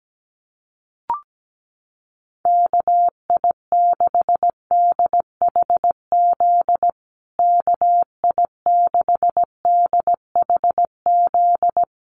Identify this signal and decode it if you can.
Morse code, heard as a keyed tone, about 17 wpm, 700 Hz. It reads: KI6DHZ KI6DHZ
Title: USA Call Signs Repeated 2x 17wpm